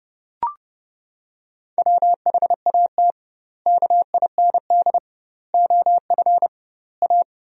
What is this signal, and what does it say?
Morse code, heard as a keyed tone, about 32 wpm, 700 Hz. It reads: WHAT KIND OF A
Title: 2x Word-Spacing - Sentences from Top 400 Words - 30wpm